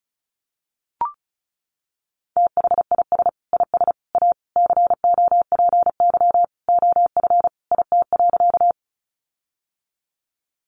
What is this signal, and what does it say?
Morse code, heard as a keyed tone, about 34 wpm, 700 Hz. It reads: THIS IS A COPY OF IT.
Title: Sentences from Top 1400 Words 35wpm